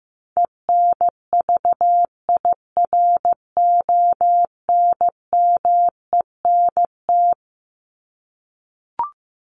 Morse code, heard as a keyed tone, about 15 wpm, 700 Hz. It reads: ENVIRONMENT
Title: Top 501-750 Words 15wpm